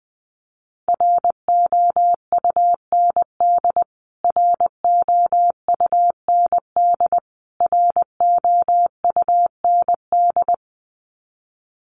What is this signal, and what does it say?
Morse code, heard as a keyed tone, about 20 wpm, 700 Hz. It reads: ROUND ROUND ROUND
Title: Top 200 Words Repeated 20wpm